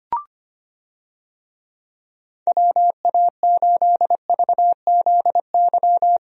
Morse code, heard as a keyed tone, about 25 wpm, 700 Hz. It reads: WA8VZY
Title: USA Call Signs - 6 Characters - Repeated 3x - 25wpm